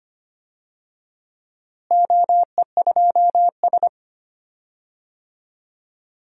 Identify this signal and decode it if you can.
Morse code, heard as a keyed tone, about 25 wpm, 700 Hz. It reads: OE2S
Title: DXCC-100 Call Signs - 4 Characters - Repeated 3x - 25wpm